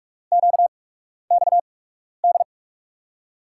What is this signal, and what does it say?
Morse code, heard as a keyed tone, about 44 wpm, spaced as for 12 wpm, 700 Hz. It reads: QXD
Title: USA Call Signs - 5 Characters - 8x Character Spacing - 45wpm